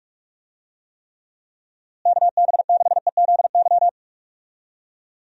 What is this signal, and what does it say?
Morse code, heard as a keyed tone, about 45 wpm, 700 Hz. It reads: KB6EZY
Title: USA Call Signs - 6 Characters - Repeated 3x - 45wpm